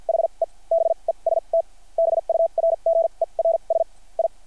Morse code, heard as a keyed tone, about 50 wpm, spaced as for 37 wpm, 660 Hz. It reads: HE BEST BURGERS I
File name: archived / CRSnet